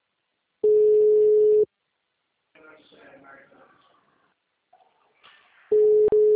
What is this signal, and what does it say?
Ringback tone (Europe, 425 Hz), with speech.